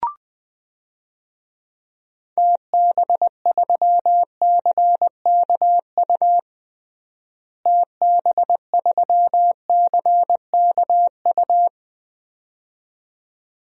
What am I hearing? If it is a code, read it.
Morse code, heard as a keyed tone, about 21 wpm, 700 Hz. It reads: TB3CKU TB3CKU
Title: DXCC-100 Call Signs - 6 Characters - Repeated 2x - 20wpm